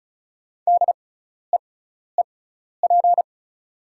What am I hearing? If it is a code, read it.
Morse code, heard as a keyed tone, about 34 wpm, spaced as for 12 wpm, 690 Hz. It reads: DEEP
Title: 6x Character Spacing - Top 500 Words - 35wpm